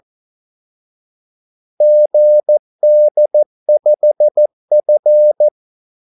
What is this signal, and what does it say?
Morse code, heard as a keyed tone, about 14 wpm, 600 Hz. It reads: GD5F
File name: Callsigns-14wpm